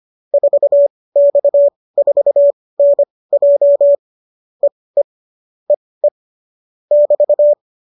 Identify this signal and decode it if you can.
Morse code, heard as a keyed tone, about 25 wpm, spaced as for 18 wpm, 570 Hz. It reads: 4X4NJ EE EE <BT>